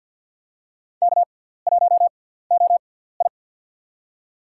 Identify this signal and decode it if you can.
Morse code, heard as a keyed tone, about 49 wpm, spaced as for 17 wpm, 700 Hz. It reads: K1OI
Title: USA Call Signs - 4 Characters - 6x Character Spacing - 50wpm